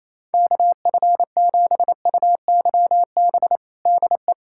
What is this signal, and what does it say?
Morse code, heard as a keyed tone, about 28 wpm, 700 Hz. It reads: KF7UYB DE
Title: USA Callsign DE Callsign - 6 Characters - 28wpm